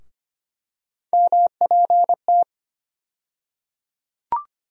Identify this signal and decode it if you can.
Morse code, heard as a keyed tone, about 25 wpm, 700 Hz. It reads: MPT
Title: USA Call Sign Postfixes - 3 Letters 25wpm